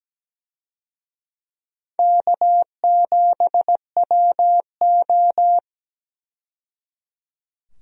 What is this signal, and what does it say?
Morse code, heard as a keyed tone, about 17 wpm, 700 Hz. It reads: K7WO
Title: USA Call Signs - 4 Characters - 17wpm